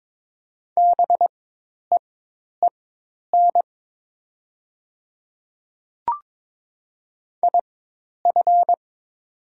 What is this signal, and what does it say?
Morse code, heard as a keyed tone, about 22 wpm, spaced as for 10 wpm, 700 Hz. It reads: BEEN IF
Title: 4x Character Spacing - Top 100 Words - 22wpm